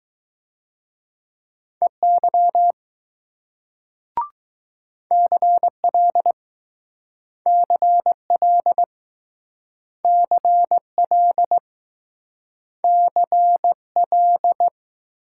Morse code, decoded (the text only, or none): EY CL CL CL CL